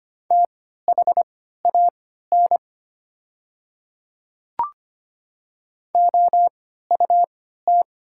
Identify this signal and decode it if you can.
Morse code, heard as a keyed tone, about 25 wpm, spaced as for 14 wpm, 700 Hz. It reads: THAN OUT
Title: 3x Character Spacing - Top 100 Words - 25wpm